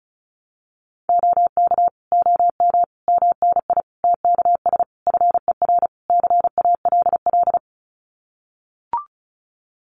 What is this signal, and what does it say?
Morse code, heard as a keyed tone, about 34 wpm, 700 Hz. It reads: OK OM MNI TKS FER CALL
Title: Common QSO Phrases 35wpm